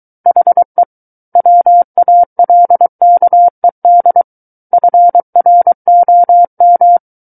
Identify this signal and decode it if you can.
Morse code, heard as a keyed tone, about 23 wpm, 700 Hz. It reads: HE WALKED FROM
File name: Book One / 14-Chapter 14 - 23 WPM